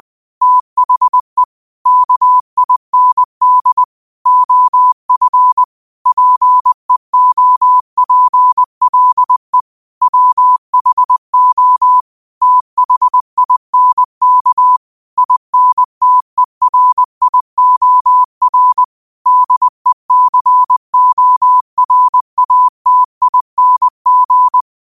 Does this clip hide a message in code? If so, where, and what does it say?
Yes: Morse code from 0.4 to 24.6 s, heard as a keyed tone, about 20 wpm, 1000 Hz. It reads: THE KIND OF PEOPLE WHO THINK INTERIOR DECORATING